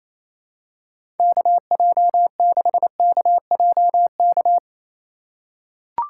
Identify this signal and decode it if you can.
Morse code, heard as a keyed tone, about 28 wpm, 700 Hz. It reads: KJ6KJK
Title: USA Call Signs - 6 Characters - 4x Character Spacing - 28wpm